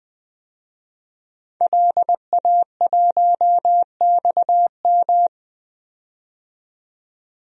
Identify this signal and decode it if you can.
Morse code, heard as a keyed tone, about 20 wpm, 700 Hz. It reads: LA1XM
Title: DXCC-100 Call Signs - 5 Characters - 20wpm